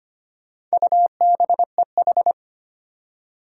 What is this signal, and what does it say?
Morse code, heard as a keyed tone, about 25 wpm, 700 Hz. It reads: UBEH
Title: Four Challenging Letter-Number Combinations 25wpm